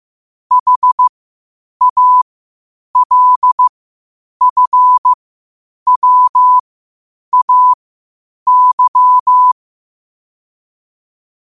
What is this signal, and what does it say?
Morse code, heard as a keyed tone, about 15 wpm, spaced as for 8 wpm, 1000 Hz. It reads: HALFWAY
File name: qotdfiles / 8wpm